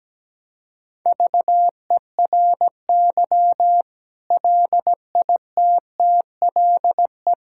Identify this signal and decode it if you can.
Morse code, heard as a keyed tone, about 17 wpm, 700 Hz. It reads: VERY LITTLE
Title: Sentences from Top 1000 Words - Review Part 6 - 17wpm